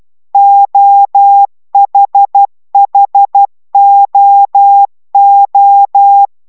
Morse code, heard as a keyed tone, about 12 wpm, 800 Hz. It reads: OHHOO